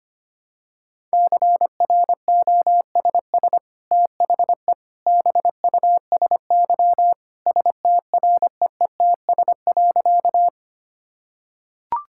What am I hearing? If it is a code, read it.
Morse code, heard as a keyed tone, about 25 wpm, 700 Hz. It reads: CROSS THE BUSY STREETS.
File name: Easy Sentences with Hints - 25wpm